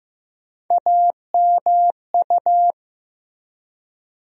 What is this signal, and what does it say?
Morse code, heard as a keyed tone, about 15 wpm, 700 Hz. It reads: AMU
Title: USA Call Sign Postfixes - 3 Letters 15wpm